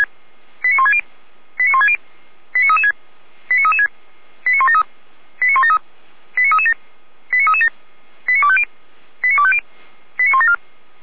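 Five-tone selective call (ZVEI), heard as quick runs of five tones, tones of about 70 ms. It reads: 89160 89160 89336 89336 89173 89173 89337 89337 89250 89250 89174
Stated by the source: Five-tone ZVEI selective call received by SDR, from the audio server of Feuerwehr Orlamünde (fire brigade).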